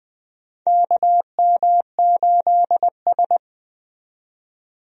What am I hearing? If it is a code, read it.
Morse code, heard as a keyed tone, about 20 wpm, 700 Hz. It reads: KM8S
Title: USA Call Signs - 4 Characters - 20wpm